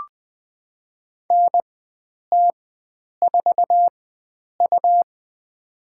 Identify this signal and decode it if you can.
Morse code, heard as a keyed tone, about 20 wpm, spaced as for 9 wpm, 700 Hz. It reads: NT4U
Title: USA Call Signs - 4 Characters - 4x Character Spacing - 20wpm